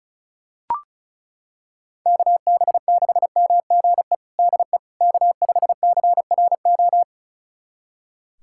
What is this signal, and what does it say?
Morse code, heard as a keyed tone, about 36 wpm, 700 Hz. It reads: KB6MGE DE K5CRO
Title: USA Callsign DE Callsign 35wpm